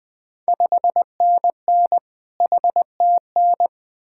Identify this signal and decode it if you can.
Morse code, heard as a keyed tone, about 20 wpm, 700 Hz. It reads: 5NN HTN
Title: Ontario QSO Party Exchange 20wpm